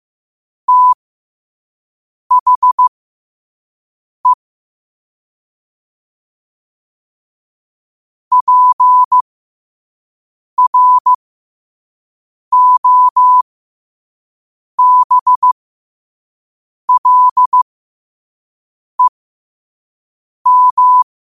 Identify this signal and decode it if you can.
Morse code, heard as a keyed tone, about 15 wpm, spaced as for 5 wpm, 1000 Hz. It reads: THE PROBLEM
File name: qotdfiles / 5wpm